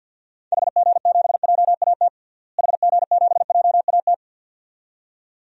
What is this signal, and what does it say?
Morse code, heard as a keyed tone, about 52 wpm, 700 Hz. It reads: HG7JAT HG7JAT
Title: DXCC-100 Call Signs - 6 Characters - Repeated 2x - 50wpm